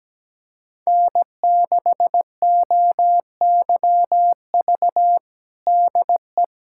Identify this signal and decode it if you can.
Morse code, heard as a keyed tone, about 17 wpm, 700 Hz. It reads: N6OYV DE
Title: USA Callsign DE Callsign - 5 Characters - 17wpm